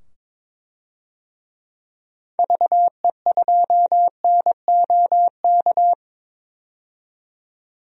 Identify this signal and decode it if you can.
Morse code, heard as a keyed tone, about 22 wpm, 700 Hz. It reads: VE2NOK
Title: DXCC-100 Call Signs - 6 Characters - 22wpm